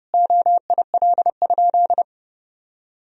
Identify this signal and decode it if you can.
Morse code, heard as a keyed tone, about 31 wpm, 700 Hz. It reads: OIL?
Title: Sentences from Top 1000 Words - Review Part 4 - 30wpm